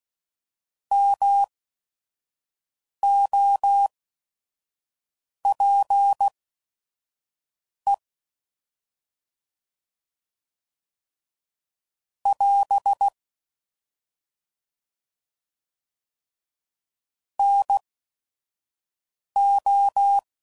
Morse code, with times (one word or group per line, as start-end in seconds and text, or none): 0.9-7.9 MOPE
12.3-13.1 <AS>
17.4-20.2 NO